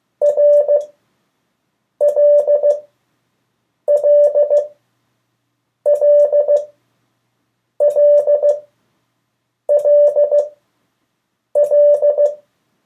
Morse code, heard as a keyed tone, about 15 wpm, spaced as for 6 wpm, 580 Hz. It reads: RLLLLLL
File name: 1of4-cw-letters